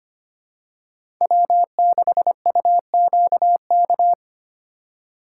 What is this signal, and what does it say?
Morse code, heard as a keyed tone, about 25 wpm, 700 Hz. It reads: W6UQK
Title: USA Call Signs - 5 Characters - 3x Character Spacing - 25wpm